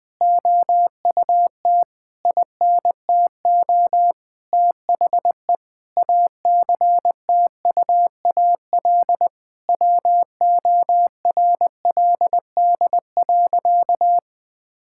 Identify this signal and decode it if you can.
Morse code, heard as a keyed tone, about 21 wpm, 700 Hz. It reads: OUT INTO THE ACTUAL WORLD.
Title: Sentences from Top 1000 Words - Review Part 5 - 20wpm